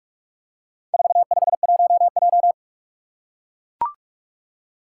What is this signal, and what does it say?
Morse code, heard as a keyed tone, about 41 wpm, 700 Hz. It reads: 451J
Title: Four Challenging Letter-Number Combinations 45wpm